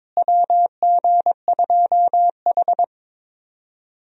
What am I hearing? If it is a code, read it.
Morse code, heard as a keyed tone, about 22 wpm, 700 Hz. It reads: WG2H
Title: USA Call Signs - 4 Characters - Repeated 2x - 22wpm